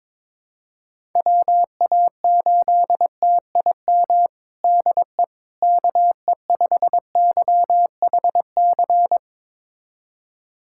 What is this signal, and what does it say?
Morse code, heard as a keyed tone, about 22 wpm, 700 Hz. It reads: WA8TIM DE KE5YHC
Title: USA Callsign DE Callsign 22wpm